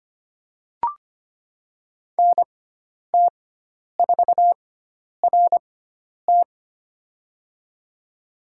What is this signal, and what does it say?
Morse code, heard as a keyed tone, about 25 wpm, spaced as for 10 wpm, 700 Hz. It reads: NT4RT
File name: Lesson 16g - Letter c - Call Signs 25wpm